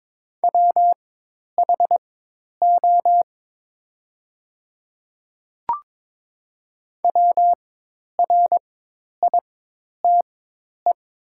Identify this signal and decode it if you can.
Morse code, heard as a keyed tone, about 22 wpm, spaced as for 10 wpm, 700 Hz. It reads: WHO WRITE
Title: 4x Character Spacing - Top 200 Words - 22wpm